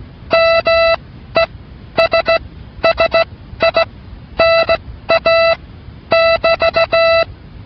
Morse code, heard as a keyed tone, about 16 wpm, spaced as for 11 wpm, 660 Hz. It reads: MESSINA <BT>